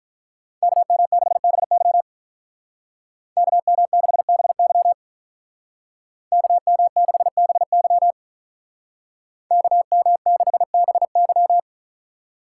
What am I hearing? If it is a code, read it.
Morse code, heard as a keyed tone, about 43 wpm, 700 Hz. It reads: KM6BY KM6BY KM6BY KM6BY
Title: Speed Racing - USA 2x2 Call Signs 35wpm